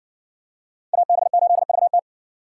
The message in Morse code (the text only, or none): AB8VT